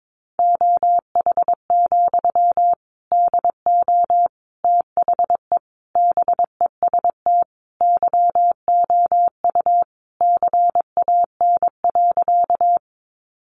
Morse code, heard as a keyed tone, about 22 wpm, 700 Hz. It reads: OH, DO THE BEST YOU CAN.
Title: Sentences from Top 500 Words 22wpm